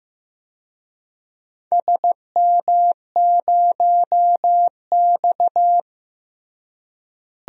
Morse code, heard as a keyed tone, about 15 wpm, 700 Hz. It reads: SM0X
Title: DXCC-100 Call Signs - 4 Characters - Repeated 2x - 15wpm